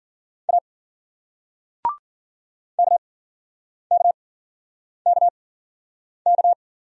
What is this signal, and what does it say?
Morse code, heard as a keyed tone, about 49 wpm, spaced as for 9 wpm, 700 Hz. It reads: A KKKK
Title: Speed-Racing Single Letter-Number 40wpm